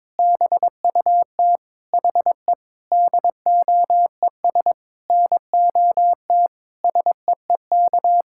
Morse code, heard as a keyed tone, about 21 wpm, 700 Hz. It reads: BUT HE DOES NOT SEEK
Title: Sentences from Top 1600 Words 22wpm